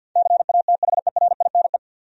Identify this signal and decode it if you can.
Morse code, heard as a keyed tone, about 51 wpm, 700 Hz. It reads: CATHERINE
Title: USA Common Names - 50wpm